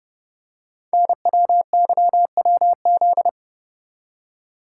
Morse code, heard as a keyed tone, about 30 wpm, 700 Hz. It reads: NWYWZ
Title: Code Groups - 5 Letters - 30wpm